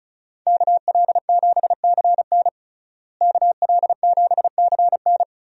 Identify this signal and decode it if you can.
Morse code, heard as a keyed tone, about 36 wpm, 700 Hz. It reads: KL7CN KL7CN
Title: USA Call Signs - 5 Characters - Repeated 2x - 35wpm